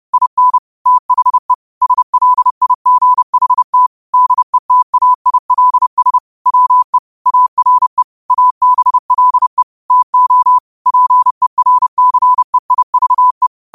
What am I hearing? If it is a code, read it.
Morse code, heard as a keyed tone, about 30 wpm, 1000 Hz. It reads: IN THE SLIGHT DETAILS WE ARE ABLE TO PERCEIVE
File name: qotdfiles / 30wpm